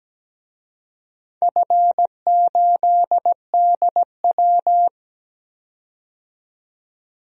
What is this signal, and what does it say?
Morse code, heard as a keyed tone, about 17 wpm, 700 Hz. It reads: F8DW
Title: DXCC-100 Call Signs - 4 Characters - 17wpm